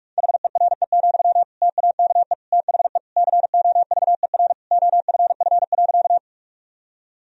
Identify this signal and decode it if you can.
Morse code, heard as a keyed tone, about 46 wpm, 700 Hz. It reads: HERE, TAKE THE COVER OFF.
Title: Sentences from Top 1000 Words - Review Part 1 - 45wpm